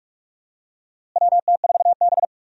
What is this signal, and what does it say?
Morse code, heard as a keyed tone, about 42 wpm, 700 Hz. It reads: WT4B
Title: USA Call Signs - 4 Characters - Repeated 3x - 45wpm